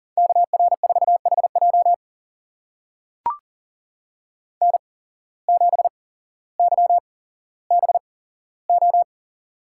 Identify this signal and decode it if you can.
Morse code, heard as a keyed tone, about 39 wpm, spaced as for 17 wpm, 700 Hz. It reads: KR4HJ N7YBO
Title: USA Call Signs - 5 Characters - 8x Character Spacing - 40wpm